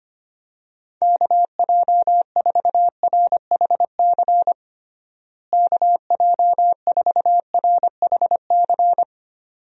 Morse code, heard as a keyed tone, about 25 wpm, 700 Hz. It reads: KJ4RHC KJ4RHC
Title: USA Call Signs - 6 Characters - Repeated 2x - 25wpm